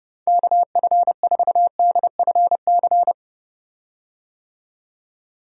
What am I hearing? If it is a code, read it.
Morse code, heard as a keyed tone, about 30 wpm, 700 Hz. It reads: KF4DFC
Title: USA Call Signs - 6 Characters - 30wpm